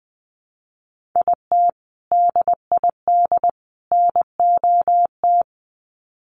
Morse code, heard as a keyed tone, about 20 wpm, 700 Hz. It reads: IT DID NOT
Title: Sets of 3 Words 20wpm